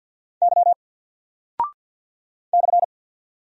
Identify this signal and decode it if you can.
Morse code, heard as a keyed tone, about 48 wpm, 700 Hz. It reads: Y /
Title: Single Character (Letter:Number:Punctuation:Prosign) Alt 50wpm